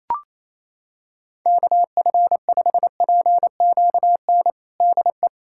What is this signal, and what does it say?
Morse code, heard as a keyed tone, about 28 wpm, 700 Hz. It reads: KF5PQN DE
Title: USA Callsign DE Callsign - 6 Characters - 28wpm